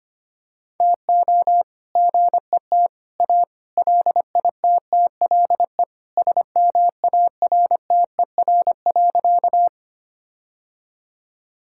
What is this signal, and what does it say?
Morse code, heard as a keyed tone, about 25 wpm, 700 Hz. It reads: TO GET A LITTLE SMARTER.
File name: practice / 25wpm / All-Age Appropriate Jokes - 25wpm